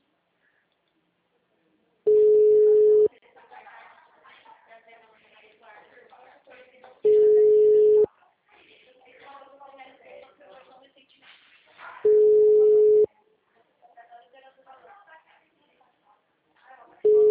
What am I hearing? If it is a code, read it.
Ringback tone (Europe, 425 Hz), with speech.
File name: out- -2005- - - .84